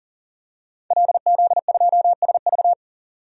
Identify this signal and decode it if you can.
Morse code, heard as a keyed tone, about 40 wpm, 700 Hz. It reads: LZ2SV
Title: DXCC-100 Call Signs - 5 Characters - 40wpm